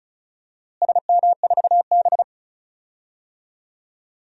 Morse code, heard as a keyed tone, about 34 wpm, 700 Hz. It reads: SM4B
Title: DXCC-100 Call Signs - 4 Characters - Repeated 3x - 35wpm